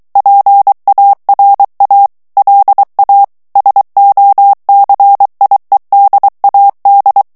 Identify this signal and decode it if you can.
Morse code, heard as a keyed tone, about 23 wpm, 800 Hz. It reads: PARA LA SOCIEDAD
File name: morse mp3 / ESPAGNOL TEXTE EN CLAIR 2 VITESSE 22 MOTS MINUTES